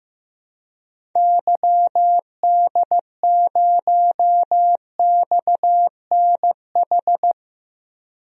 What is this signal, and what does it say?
Morse code, heard as a keyed tone, about 15 wpm, 700 Hz. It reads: YD0XNH
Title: DXCC-100 Call Signs - 6 Characters - 15wpm